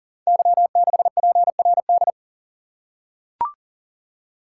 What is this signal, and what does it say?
Morse code, heard as a keyed tone, about 41 wpm, 690 Hz. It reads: Y6PRD
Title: Code Groups - 5 Letter:Numbers - 40wpm